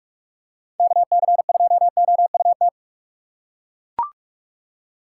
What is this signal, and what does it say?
Morse code, heard as a keyed tone, about 41 wpm, 700 Hz. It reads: KC2OUT
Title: USA Call Signs - 6 Characters - 2x Character Spacing - 45wpm